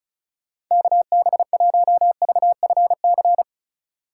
Morse code, heard as a keyed tone, about 37 wpm, 700 Hz. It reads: KB1VFC
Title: Speed Racing - USA 2x3 Call Signs 30wpm